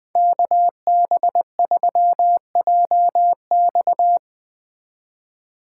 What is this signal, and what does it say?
Morse code, heard as a keyed tone, about 20 wpm, 700 Hz. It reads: KB3JX
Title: Speed Racing - USA 2x2 Call Signs 17wpm